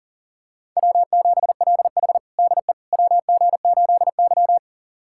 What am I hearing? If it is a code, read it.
Morse code, heard as a keyed tone, about 39 wpm, 700 Hz. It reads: W7LH DE WG8Y
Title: USA Callsign DE Callsign - 4 Characters - 40wpm